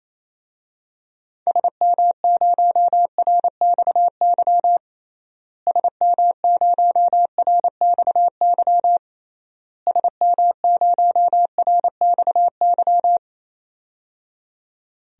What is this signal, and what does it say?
Morse code, heard as a keyed tone, about 27 wpm, 700 Hz. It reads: SM0RXY SM0RXY SM0RXY
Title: DXCC-100 Call Signs - 6 Characters - Repeated 3x - 28wpm